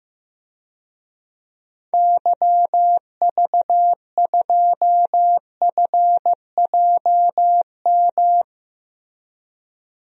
Morse code, heard as a keyed tone, about 16 wpm, 700 Hz. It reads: YV2FJM
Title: DXCC-100 Call Signs - 6 Characters - Repeated 2x - 15wpm